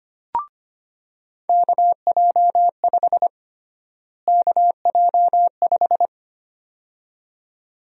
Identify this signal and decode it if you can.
Morse code, heard as a keyed tone, about 25 wpm, 700 Hz. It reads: KJ5 KJ5
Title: USA Call Sign Prefixes - Repeated 2x - 25wpm